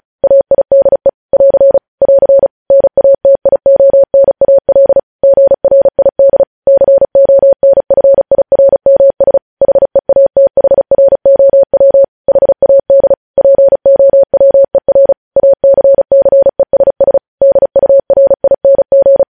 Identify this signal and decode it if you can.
Morse code, heard as a keyed tone, about 36 wpm, 550 Hz. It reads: AIDE <AR> <AR> NATIONAL GRID CONFIRMS HEATHROW HAD POWER ACCESS DURING
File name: podcast / News headlines-1x-35x35